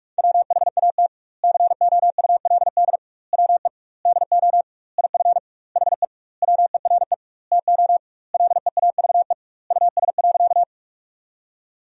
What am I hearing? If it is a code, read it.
Morse code, heard as a keyed tone, about 41 wpm, 700 Hz. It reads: WHAT COULD WE DO IF HE WERE TO LEAVE US.
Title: 2x Word-Spacing - Sentences from Top 400 Words - 45wpm